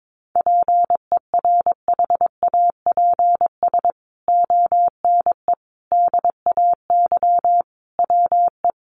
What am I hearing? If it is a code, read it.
Morse code, heard as a keyed tone, about 22 wpm, 700 Hz. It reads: PERHAPS ONE DAY WE
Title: Sentences from Top 600 Words 22wpm